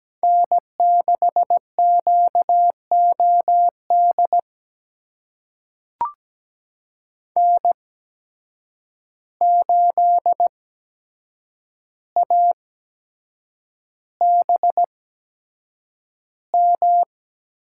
Morse code, heard as a keyed tone, about 17 wpm, spaced as for 7 wpm, 700 Hz. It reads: N6QOD N8ABM
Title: USA Call Signs - 5 Characters - 8x Character Spacing - 17wpm